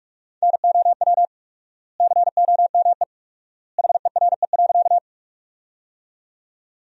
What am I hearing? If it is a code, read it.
Morse code, heard as a keyed tone, about 42 wpm, 700 Hz. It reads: NOW COME HERE.
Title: 4x Word-Spacing - Sentences from Top 200 Words - 45wpm